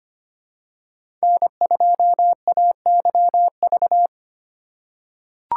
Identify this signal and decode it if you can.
Morse code, heard as a keyed tone, about 25 wpm, 700 Hz. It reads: N2AYV
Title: USA Call Signs - 4x Character Spacing - 25wpm